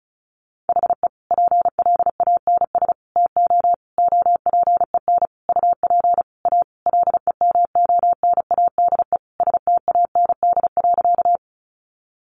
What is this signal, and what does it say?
Morse code, heard as a keyed tone, about 34 wpm, 700 Hz. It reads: HE PLANS TO OPEN UP A LEMONADE STAND.
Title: Additional Easy Phrases 35wpm